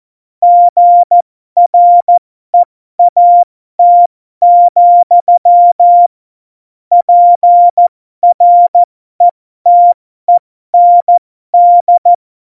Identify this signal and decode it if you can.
Morse code, heard as a keyed tone, about 14 wpm, spaced as for 12 wpm, 700 Hz. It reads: GREAT, PRETEND
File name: Morsekurs / Witze